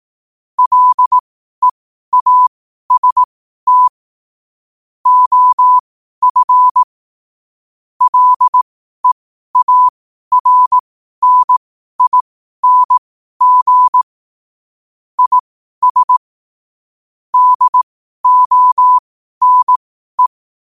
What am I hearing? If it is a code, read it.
Morse code, heard as a keyed tone, about 18 wpm, spaced as for 12 wpm, 1000 Hz. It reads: LEAST OF LEARNING IS DONE
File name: qotdfiles / 12wpm